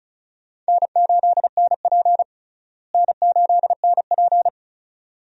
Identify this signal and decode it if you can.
Morse code, heard as a keyed tone, about 34 wpm, 700 Hz. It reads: N8NP N8NP
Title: USA Call Signs - 4 Characters - Repeated 2x - 35wpm